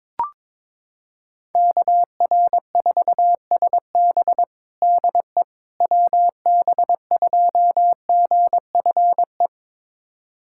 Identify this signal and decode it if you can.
Morse code, heard as a keyed tone, about 22 wpm, 700 Hz. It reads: KR4SB DE WB2GFE
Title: USA Callsign DE Callsign 22wpm